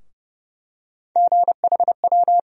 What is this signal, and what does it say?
Morse code, heard as a keyed tone, about 31 wpm, 700 Hz. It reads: GHW
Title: USA Call Sign Postfixes - 3 Letters 30wpm